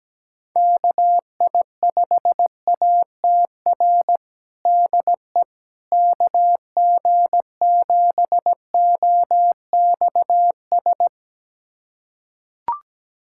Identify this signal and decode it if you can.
Morse code, heard as a keyed tone, about 17 wpm, 700 Hz. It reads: KI5ATR DE KG7OXS
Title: USA Callsign DE Callsign 17wpm